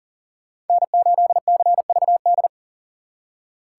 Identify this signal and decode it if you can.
Morse code, heard as a keyed tone, about 52 wpm, 700 Hz. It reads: N8CVD